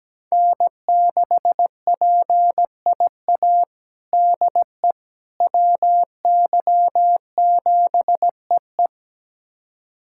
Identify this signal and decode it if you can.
Morse code, heard as a keyed tone, about 17 wpm, 700 Hz. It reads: N6PIA DE WY7EE
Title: USA Callsign DE Callsign - 5 Characters - 17wpm